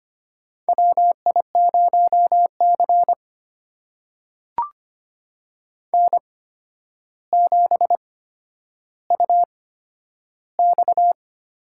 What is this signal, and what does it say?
Morse code, heard as a keyed tone, about 25 wpm, spaced as for 11 wpm, 700 Hz. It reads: WI0C N7UX
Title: USA Call Signs - 4 Characters - 8x Character Spacing - 25wpm